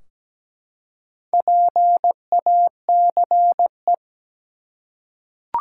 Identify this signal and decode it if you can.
Morse code, heard as a keyed tone, about 17 wpm, 700 Hz. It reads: PACE
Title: Four Letter Words - 17wpm